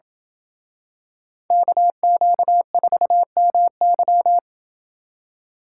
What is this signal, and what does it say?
Morse code, heard as a keyed tone, about 27 wpm, 700 Hz. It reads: KQ4MY